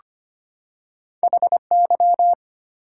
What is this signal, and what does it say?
Morse code, heard as a keyed tone, about 25 wpm, 700 Hz. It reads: HY
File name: Two Letter-Number Combinations - 25wpm